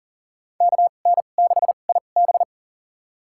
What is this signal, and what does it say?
Morse code, heard as a keyed tone, about 40 wpm, spaced as for 29 wpm, 700 Hz. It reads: KN6IB
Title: USA Call Signs - 5 Characters - 2x Character Spacing - 40wpm